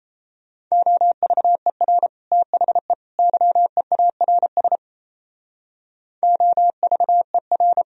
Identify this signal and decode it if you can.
Morse code, heard as a keyed tone, about 34 wpm, 700 Hz. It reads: OVER THE YEARS OVER
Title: Speed-Racing - Sets of 3 Words 28wpm